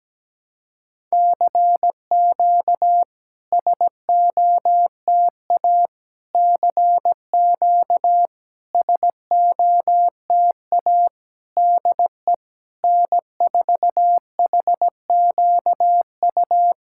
Morse code, heard as a keyed tone, about 17 wpm, 690 Hz. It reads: CQ SOTA CQ SOTA DE N4HQU
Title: SOTA QSO Practice 17wpm